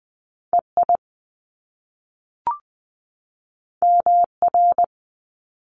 Morse code, heard as a keyed tone, about 20 wpm, 700 Hz. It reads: EI MR
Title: Top Two Letter Combinations in Words 20wpm